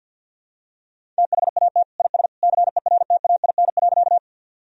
Morse code, heard as a keyed tone, about 53 wpm, 700 Hz. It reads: THAT IS CERTAIN.
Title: Sentences from Top 1000 Words - Review Part 2 - 50wpm